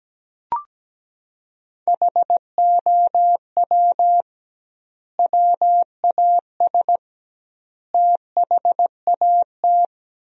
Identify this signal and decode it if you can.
Morse code, heard as a keyed tone, about 17 wpm, 700 Hz. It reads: HOW WAS THAT
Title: 2x Word-Spacing - Sentences from Top 500 Words - 17wpm